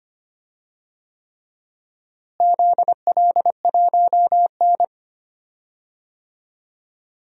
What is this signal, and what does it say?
Morse code, heard as a keyed tone, about 25 wpm, 700 Hz. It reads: ZL1N
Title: DXCC-100 Call Signs - 4 Characters - Repeated 3x - 25wpm